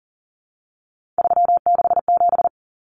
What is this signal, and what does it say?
Morse code, heard as a keyed tone, about 40 wpm, 700 Hz. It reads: 367
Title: Common Cut Numbers - 3-Digit 40wpm